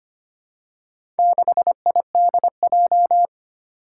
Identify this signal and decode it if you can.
Morse code, heard as a keyed tone, about 25 wpm, 700 Hz. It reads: 6IDJ
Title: Four Challenging Letter-Number Combinations 25wpm